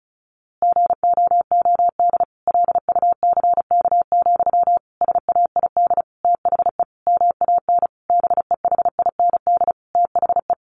Morse code, heard as a keyed tone, about 34 wpm, 700 Hz. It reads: GOOD LUCK, SAID THE MAN BEHIND THE
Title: A Sound Of Thunder 35wpm